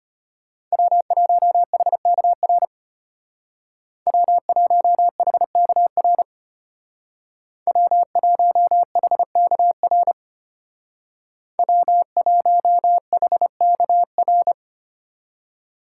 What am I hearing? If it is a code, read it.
Morse code, heard as a keyed tone, about 33 wpm, 700 Hz. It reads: W1HKR W1HKR W1HKR W1HKR